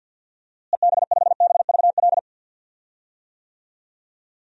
Morse code, heard as a keyed tone, about 52 wpm, 700 Hz. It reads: EB5BVL
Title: DXCC-100 Call Signs - 6 Characters - Repeated 2x - 50wpm